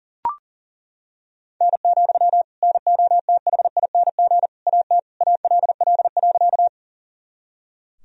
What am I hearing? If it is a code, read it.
Morse code, heard as a keyed tone, about 40 wpm, 700 Hz. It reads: N, NOTHING AT ALL.
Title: Sentences from Top 500 Words and Abbreviations 40wpm